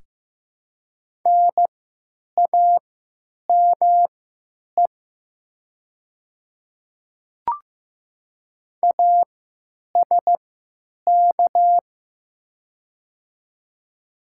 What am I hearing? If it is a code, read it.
Morse code, heard as a keyed tone, about 15 wpm, spaced as for 9 wpm, 700 Hz. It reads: NAME ASK
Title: 3x Character Spacing - Top 200 Words - 15wpm